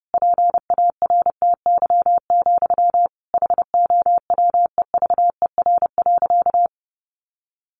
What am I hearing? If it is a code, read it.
Morse code, heard as a keyed tone, about 30 wpm, 700 Hz. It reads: PARTY, HOWEVER.
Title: Sentences from Top 1400 Words 30wpm